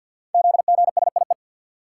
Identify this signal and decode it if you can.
Morse code, heard as a keyed tone, about 51 wpm, 700 Hz. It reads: ZGSIE
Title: Code Groups - 5 Letters - 50wpm